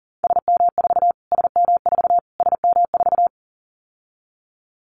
Morse code, heard as a keyed tone, about 40 wpm, 700 Hz. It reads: SM4 SM4 SM4
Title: Callsign Prefixes for Top 25 DXCC - Repeated 3x - 40wpm